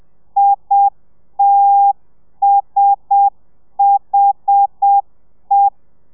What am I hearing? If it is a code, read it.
Morse code, heard as a keyed tone, about 7 wpm, 800 Hz. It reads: ITSHE
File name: telegrafia / i2tfj / Lez1 30